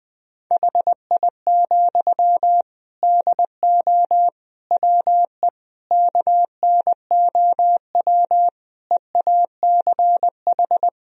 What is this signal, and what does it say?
Morse code, heard as a keyed tone, about 20 wpm, 700 Hz. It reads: HI, DO WE KNOW EACH
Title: Sentences from Top 1000 Words - Review Part 3 - 20wpm